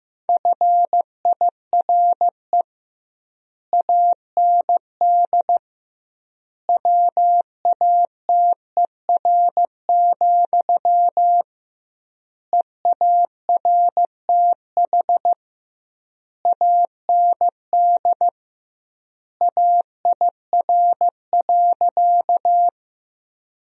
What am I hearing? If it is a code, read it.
Morse code, heard as a keyed tone, about 15 wpm, 700 Hz. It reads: FIRE AND WATER, EARTH AND AIR.
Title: 2x Word-Spacing - Sentences from Top 400 Words - 15wpm